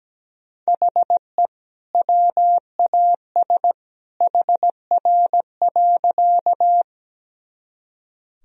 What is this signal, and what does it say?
Morse code, heard as a keyed tone, about 17 wpm, 700 Hz. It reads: HE WAS HR.
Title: Sentences from Top 200 Words and Abbreviations 17wpm